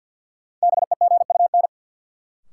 Morse code, heard as a keyed tone, about 51 wpm, 700 Hz. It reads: BEGUN
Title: Sentences from Top 1000 Words - Review Part 6 - New Word Drill - 50wpm